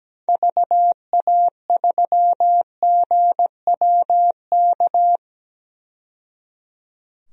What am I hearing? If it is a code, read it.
Morse code, heard as a keyed tone, about 17 wpm, 700 Hz. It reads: VA3GWK
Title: DXCC-100 Call Signs - 6 Characters - 17wpm